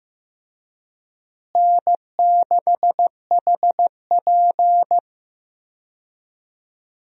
Morse code, heard as a keyed tone, about 15 wpm, 700 Hz. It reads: N6HP
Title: USA Call Signs - 4 Characters - Repeated 2x - 15wpm